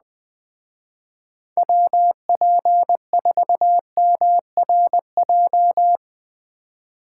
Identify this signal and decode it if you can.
Morse code, heard as a keyed tone, about 20 wpm, 700 Hz. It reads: WP4MRJ
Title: Speed Racing - USA 2x3 Call Signs 17wpm